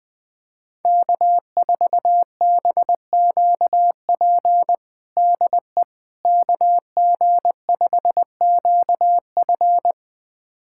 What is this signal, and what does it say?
Morse code, heard as a keyed tone, about 20 wpm, 700 Hz. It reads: K4BQP DE KG5QF
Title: USA Callsign DE Callsign 20wpm